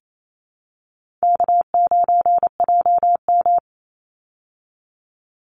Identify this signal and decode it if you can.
Morse code, heard as a keyed tone, about 28 wpm, 700 Hz. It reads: K9JM
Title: Easy Call Signs Repeated 3x - 28wpm